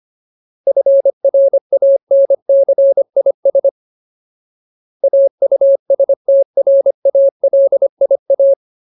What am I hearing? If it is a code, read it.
Morse code, heard as a keyed tone, about 25 wpm, 550 Hz. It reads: FRANCIS AUSTRALIA
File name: Ultimate Word List / 25WPM Ultimate